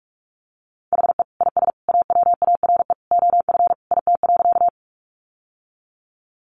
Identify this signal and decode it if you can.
Morse code, heard as a keyed tone, about 44 wpm, 700 Hz. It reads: HE IS AWARE OF IT.
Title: Sentences from Top 1200 Words 45wpm